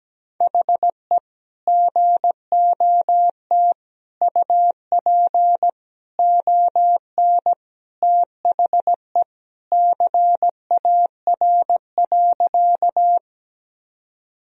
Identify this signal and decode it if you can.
Morse code, heard as a keyed tone, about 17 wpm, 700 Hz. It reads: HE GOT UP ON THE CAR.
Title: Sentences from Top 1000 Words - Review Part 1 - 17wpm